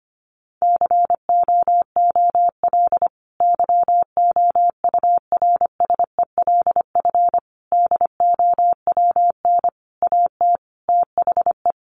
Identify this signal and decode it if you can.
Morse code, heard as a keyed tone, about 25 wpm, 700 Hz. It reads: COOL YOURSELF DOWN AT THE
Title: Sentences from Top 1200 Words 25wpm